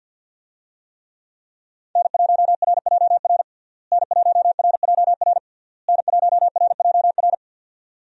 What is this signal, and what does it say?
Morse code, heard as a keyed tone, about 52 wpm, 700 Hz. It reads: N1RJR N1RJR N1RJR
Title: USA Call Signs - 5 Characters - Repeated 3x - 50wpm